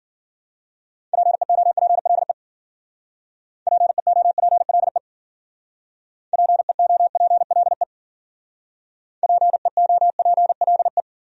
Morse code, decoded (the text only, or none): PEOPLE PEOPLE PEOPLE PEOPLE